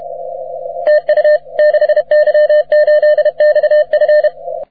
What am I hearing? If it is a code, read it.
Morse code, heard as a keyed tone, about 31 wpm, 600 Hz. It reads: TU 6Y8XF
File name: wp / 6y8xf 30m cw